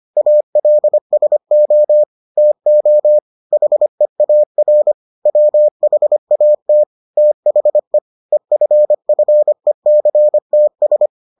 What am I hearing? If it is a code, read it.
Morse code, heard as a keyed tone, about 25 wpm, 600 Hz. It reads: ALSO TO HEAR WHAT THE EFFECTS